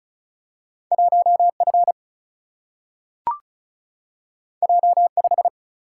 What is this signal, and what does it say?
Morse code, heard as a keyed tone, about 36 wpm, 700 Hz. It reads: 1F J5
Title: Single-Letter Single-Number 35wpm